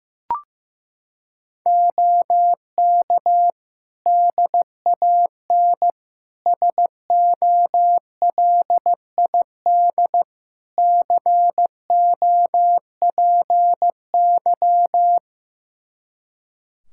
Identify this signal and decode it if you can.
Morse code, heard as a keyed tone, about 15 wpm, 700 Hz. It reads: OK DAN SOLID COPY
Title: Practice QSO - Exchange 3 - More Info - 15wpm